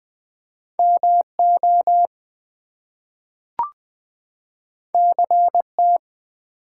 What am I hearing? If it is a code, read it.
Morse code, heard as a keyed tone, about 20 wpm, 700 Hz. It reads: MO CT
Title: US States and Canadian Provinces 20wpm